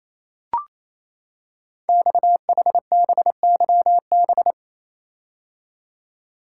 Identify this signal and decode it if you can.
Morse code, heard as a keyed tone, about 27 wpm, 700 Hz. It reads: XHBYB
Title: Code Groups - 5 Letters - 28wpm